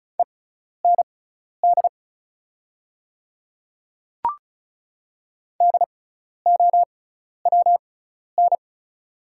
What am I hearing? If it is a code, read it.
Morse code, heard as a keyed tone, about 36 wpm, spaced as for 12 wpm, 700 Hz. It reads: END DOWN
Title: 6x Character Spacing - Top 300 Words - 35wpm